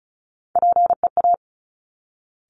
Morse code, heard as a keyed tone, about 36 wpm, 700 Hz. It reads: PEA
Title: Top Three Letter Combinations in Words 35wpm